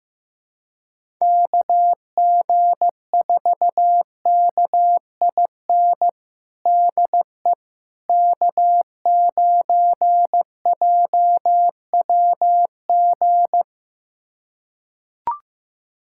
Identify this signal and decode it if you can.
Morse code, heard as a keyed tone, about 15 wpm, 700 Hz. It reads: KG4KIN DE K9JWG
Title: USA Callsign DE Callsign 15wpm